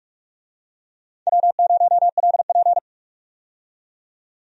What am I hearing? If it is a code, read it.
Morse code, heard as a keyed tone, about 43 wpm, 700 Hz. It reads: W0LP
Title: USA Call Signs - 4 Characters - 45wpm